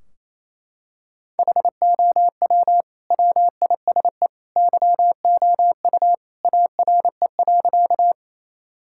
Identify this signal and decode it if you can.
Morse code, heard as a keyed tone, about 28 wpm, 700 Hz. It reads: HOW WISE YOU ARE.
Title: Sentences from Top 1600 Words 28wpm